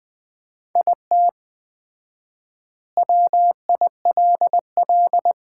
Morse code, heard as a keyed tone, about 21 wpm, 700 Hz. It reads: IT WILL
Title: 4x Word-Spacing - Sets of 2 Words - 20wpm